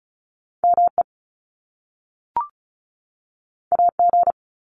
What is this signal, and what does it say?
Morse code, heard as a keyed tone, about 36 wpm, 700 Hz. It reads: ME AG
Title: Top Two Letter Combinations in Words 35wpm